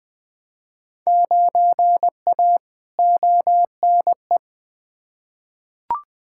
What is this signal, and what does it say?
Morse code, heard as a keyed tone, about 20 wpm, 700 Hz. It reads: 9A ONE
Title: Field Day Exchanges Encore - 20wpm